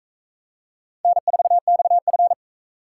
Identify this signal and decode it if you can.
Morse code, heard as a keyed tone, about 39 wpm, 700 Hz. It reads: N4XF